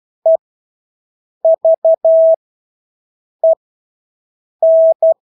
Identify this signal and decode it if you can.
Morse code, heard as a keyed tone, about 12 wpm, spaced as for 6 wpm, 650 Hz. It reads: EVEN
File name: Engelske ord 12-6 wpm